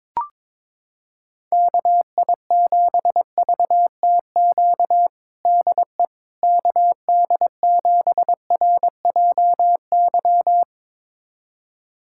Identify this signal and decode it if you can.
Morse code, heard as a keyed tone, about 22 wpm, 700 Hz. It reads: KI7VTQ DE KD7RJY
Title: USA Callsign DE Callsign - 6 Characters - 22wpm